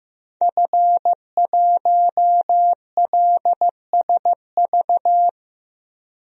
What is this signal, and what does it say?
Morse code, heard as a keyed tone, about 15 wpm, 700 Hz. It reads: F1LSV
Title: DXCC-100 Call Signs - 5 Characters - Repeated 2x - 15wpm